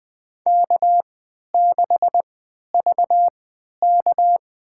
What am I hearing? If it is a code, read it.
Morse code, heard as a keyed tone, about 20 wpm, spaced as for 11 wpm, 700 Hz. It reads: K6VK
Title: USA Call Signs - 3x Character Spacing - 20wpm